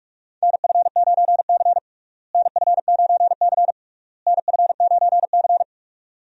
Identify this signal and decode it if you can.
Morse code, heard as a keyed tone, about 41 wpm, 700 Hz. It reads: NF9C NF9C NF9C
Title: USA Call Signs - 4 Characters - Repeated 3x - 45wpm